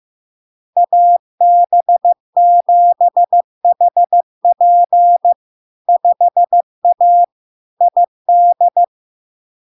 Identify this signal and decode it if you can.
Morse code, heard as a keyed tone, about 15 wpm, 700 Hz. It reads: AB7HP 5A ID
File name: LARK-FD-2022-Log-in-CW-15wpm-700-Hz